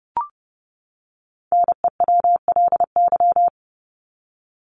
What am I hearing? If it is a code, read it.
Morse code, heard as a keyed tone, about 30 wpm, 700 Hz. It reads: NEWLY
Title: LY Suffix - Top 100 Words with Base Words 30wpm